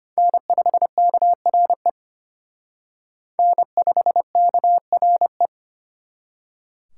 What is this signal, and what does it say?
Morse code, heard as a keyed tone, about 32 wpm, 700 Hz. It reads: N5KRE N5KRE